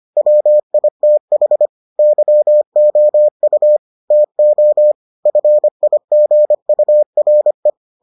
Morse code, heard as a keyed tone, about 25 wpm, 600 Hz. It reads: WITH YOU TO FIGURE